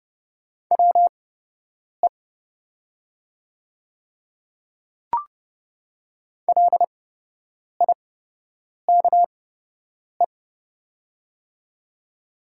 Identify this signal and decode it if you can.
Morse code, heard as a keyed tone, about 29 wpm, spaced as for 8 wpm, 690 Hz. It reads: WE LIKE
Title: 8x Character Spacing - Top 200 Words - 30wpm